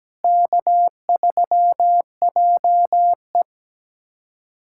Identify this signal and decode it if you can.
Morse code, heard as a keyed tone, about 17 wpm, 700 Hz. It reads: K3JE
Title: USA Callsign DE Callsign - 4 Characters - 17wpm